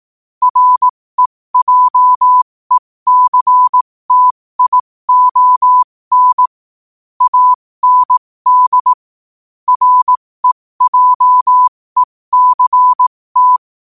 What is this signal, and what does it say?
Morse code, heard as a keyed tone, about 18 wpm, spaced as for 15 wpm, 1000 Hz. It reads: REJECTION AND REJECT
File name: qotdfiles / 15wpm